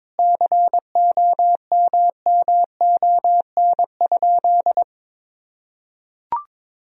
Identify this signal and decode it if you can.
Morse code, heard as a keyed tone, about 22 wpm, 700 Hz. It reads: COMMON?
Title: Sentences from Top 1000 Words - Review Part 3 - 22wpm